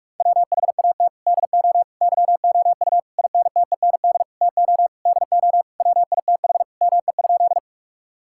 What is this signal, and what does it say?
Morse code, heard as a keyed tone, about 46 wpm, 700 Hz. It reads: WHAT DO YOU INTEND TO DO WITH ME?
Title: Sentences from Top 1600 Words 45wpm